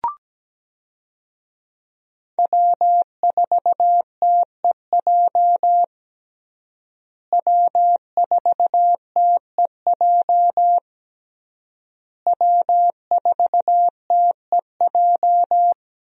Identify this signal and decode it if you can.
Morse code, heard as a keyed tone, about 17 wpm, 700 Hz. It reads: W4TEJ W4TEJ W4TEJ
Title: USA Call Signs - 5 Characters - Repeated 3x - 17wpm